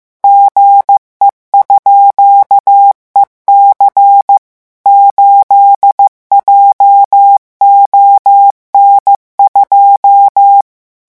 Morse code, heard as a keyed tone, about 15 wpm, 800 Hz. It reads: GE_EC 8JON2
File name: morse mp3 / revision17-vitesse 14 mots